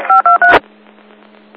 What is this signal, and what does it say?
DTMF, heard as telephone touch tones, tones of about 130 ms. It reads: 523
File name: Old Forge Group 2017 08 26 13 52 31